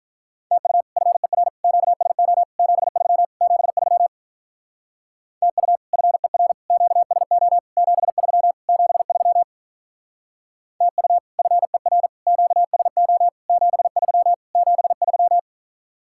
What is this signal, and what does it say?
Morse code, heard as a keyed tone, about 43 wpm, 700 Hz. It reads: TU FER QSO 73 73 TU FER QSO 73 73 TU FER QSO 73 73
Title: Speed-Racing Common QSO Phrases Encore - 35wpm